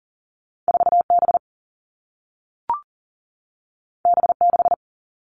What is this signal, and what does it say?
Morse code, heard as a keyed tone, about 39 wpm, 700 Hz. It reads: 4 B B 6
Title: Two Challenging Letter-Number Combinations 40wpm